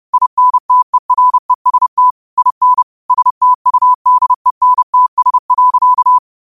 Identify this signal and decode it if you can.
Morse code, heard as a keyed tone, about 30 wpm, 1000 Hz. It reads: INTEREST IN STUDENTS.
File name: qotdfiles / 30wpm